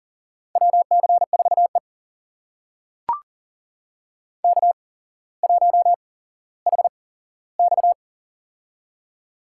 Morse code, heard as a keyed tone, about 40 wpm, spaced as for 17 wpm, 700 Hz. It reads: WC4E K1HX
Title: USA Call Signs - 4 Characters - 8x Character Spacing - 40wpm